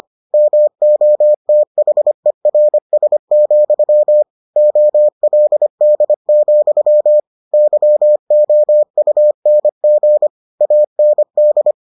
Morse code, heard as a keyed tone, about 25 wpm, 600 Hz. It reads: MOTHERS, OLD, YOUNG AND